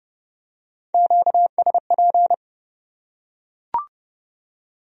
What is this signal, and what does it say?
Morse code, heard as a keyed tone, about 29 wpm, 700 Hz. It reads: QSP